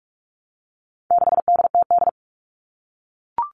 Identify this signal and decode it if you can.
Morse code, heard as a keyed tone, about 44 wpm, 700 Hz. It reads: 6DTD
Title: All Cut Numbers - 4-Digit 45wpm